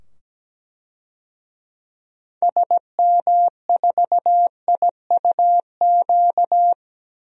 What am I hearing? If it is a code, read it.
Morse code, heard as a keyed tone, about 17 wpm, 700 Hz. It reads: SM4IUQ
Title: DXCC-100 Call Signs - 6 Characters - 17wpm